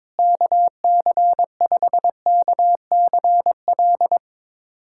Morse code, heard as a keyed tone, about 22 wpm, 700 Hz. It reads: KC5KCL
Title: USA Callsign DE Callsign 22wpm